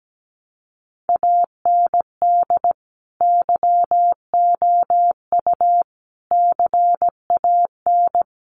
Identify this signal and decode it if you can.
Morse code, heard as a keyed tone, about 17 wpm, 700 Hz. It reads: AND YOU CAN
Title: Sentences from Top 400 Words 17wpm